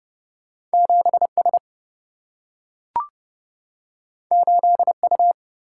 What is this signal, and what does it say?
Morse code, heard as a keyed tone, about 30 wpm, 700 Hz. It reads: 7S 8U
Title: Single-Letter Single-Number 30wpm